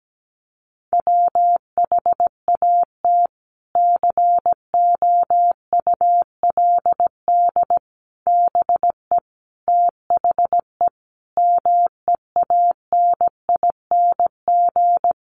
Morse code, heard as a keyed tone, about 17 wpm, 700 Hz. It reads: WHAT COULD BE THE MEANING
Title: Sentences from Top 900 Words 17wpm